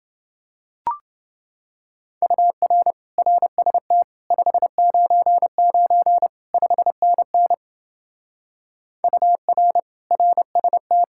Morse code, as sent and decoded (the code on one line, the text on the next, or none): ..- .-. / .-. ... - / ..... ----. ----. / ..... -. -. / ..- .-. / .-. ... -
UR RST 599 5NN UR RST